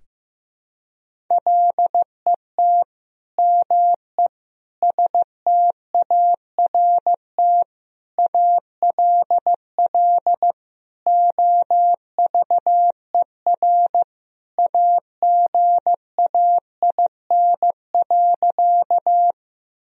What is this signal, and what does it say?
Morse code, heard as a keyed tone, about 15 wpm, 700 Hz. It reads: LET ME START ALL OVER AGAIN.
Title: Sentences from Top 1000 Words - Review Part 1 - 15wpm